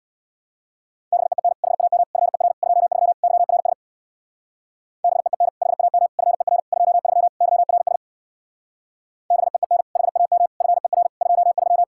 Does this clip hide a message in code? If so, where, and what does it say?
Yes: Morse code from 1.1 to 11.8 s, heard as a keyed tone, about 66 wpm, 700 Hz. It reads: BEEN HAM FER 23 YRS BEEN HAM FER 23 YRS BEEN HAM FER 23